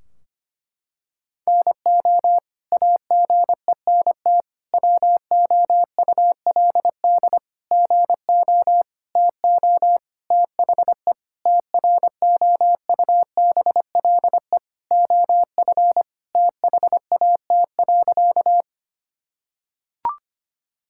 Morse code, heard as a keyed tone, about 25 wpm, 700 Hz. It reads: NO AGENT WOULD GO TO THE TROUBLE OF THAT.
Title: Sentences from Top 1600 Words 25wpm